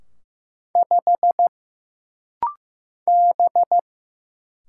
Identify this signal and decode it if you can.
Morse code, heard as a keyed tone, about 15 wpm, 700 Hz. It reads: 5 B
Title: Single Letter-Number Alt - Rapid-Fire 15wpm